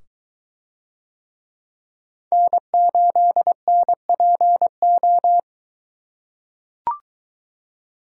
Morse code, heard as a keyed tone, about 23 wpm, 700 Hz. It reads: N8NPO